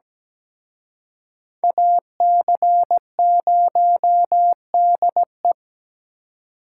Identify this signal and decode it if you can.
Morse code, heard as a keyed tone, about 20 wpm, 700 Hz. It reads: AC0DE
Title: Speed Racing - USA Call Signs 17wpm